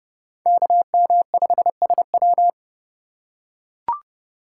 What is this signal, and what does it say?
Morse code, heard as a keyed tone, about 30 wpm, 700 Hz. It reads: KM5SW